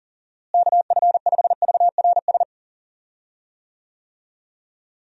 Morse code, heard as a keyed tone, about 40 wpm, 700 Hz. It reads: KF5VRS
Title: USA Call Signs - 6 Characters - Repeated 2x - 40wpm